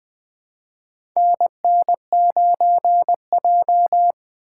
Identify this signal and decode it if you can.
Morse code, heard as a keyed tone, about 20 wpm, 700 Hz. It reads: NN9J
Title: USA Call Signs - 4 Characters - 4x Character Spacing - 20wpm